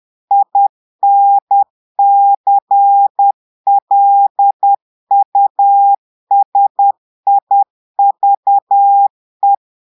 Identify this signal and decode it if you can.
Morse code, heard as a keyed tone, about 10 wpm, 800 Hz. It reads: INCLUSIVE
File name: portugais texte en clair 2 vitesse 10 mots minutes